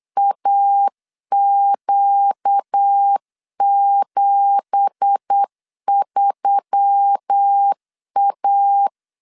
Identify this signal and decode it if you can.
Morse code, heard as a keyed tone, about 8 wpm, 800 Hz. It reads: AQ73A
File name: CW / DET / 1 9 1500C S B 8wpm 5624